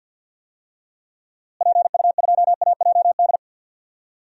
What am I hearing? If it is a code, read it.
Morse code, heard as a keyed tone, about 50 wpm, 700 Hz. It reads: PU2AJD
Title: DXCC-100 Call Signs - 6 Characters - Repeated 3x - 50wpm